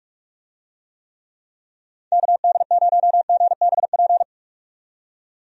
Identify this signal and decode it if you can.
Morse code, heard as a keyed tone, about 44 wpm, 700 Hz. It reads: KD0GBP
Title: USA Call Signs - 6 Characters - 45wpm